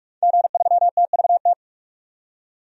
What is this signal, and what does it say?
Morse code, heard as a keyed tone, about 47 wpm, 700 Hz. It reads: G3TVT
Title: DXCC-100 Call Signs - 5 Characters - 45wpm